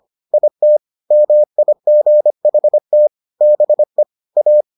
Morse code, heard as a keyed tone, about 25 wpm, 600 Hz. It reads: IT MIGHT BE A